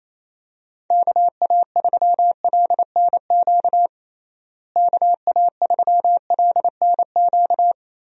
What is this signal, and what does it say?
Morse code, heard as a keyed tone, about 28 wpm, 700 Hz. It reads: KA3LNQ KA3LNQ
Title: USA Call Signs - 6 Characters - Repeated 2x - 28wpm